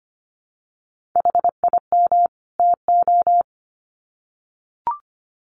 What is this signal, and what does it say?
Morse code, heard as a keyed tone, about 25 wpm, 700 Hz. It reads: HIM TO
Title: Sets of 2 Words 25wpm